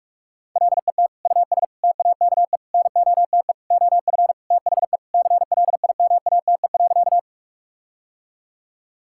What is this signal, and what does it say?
Morse code, heard as a keyed tone, about 46 wpm, 700 Hz. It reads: LET US TAKE NOTE OF THE CLIMATE.
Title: Sentences from Top 1600 Words 45wpm